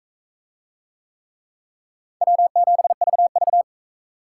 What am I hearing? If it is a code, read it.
Morse code, heard as a keyed tone, about 41 wpm, 700 Hz. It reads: W7VV